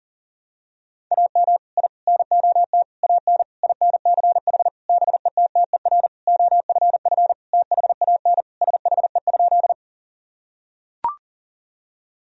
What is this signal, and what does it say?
Morse code, heard as a keyed tone, about 40 wpm, 700 Hz. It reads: AM I NOT AN INCH BETTER OFF THAN SHE?
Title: Sentences from Top 1000 Words - Review Part 3 - 40wpm